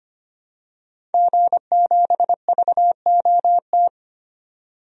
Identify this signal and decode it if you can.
Morse code, heard as a keyed tone, about 25 wpm, 700 Hz. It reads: G7VOT
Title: DXCC-100 Call Signs - 5 Characters - 25wpm